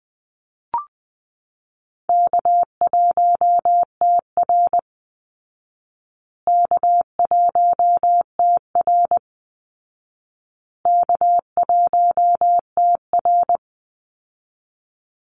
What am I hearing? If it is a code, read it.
Morse code, heard as a keyed tone, about 19 wpm, 700 Hz. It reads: K1TR K1TR K1TR
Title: Easy Call Signs Repeated 3x - 20wpm